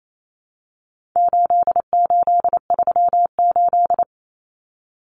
Morse code, heard as a keyed tone, about 28 wpm, 700 Hz. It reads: 8838
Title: Numbers - 4-Digit 28wpm